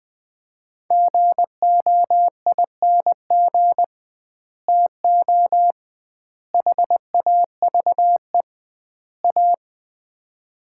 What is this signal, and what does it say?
Morse code, heard as a keyed tone, about 20 wpm, 700 Hz. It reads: GOING TO HAVE A
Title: 2x Word-Spacing - Sets of 4 Words - 20wpm